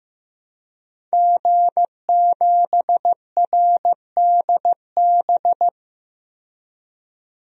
Morse code, heard as a keyed tone, about 15 wpm, 700 Hz. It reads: G7RDB
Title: DXCC-100 Call Signs - 5 Characters - 15wpm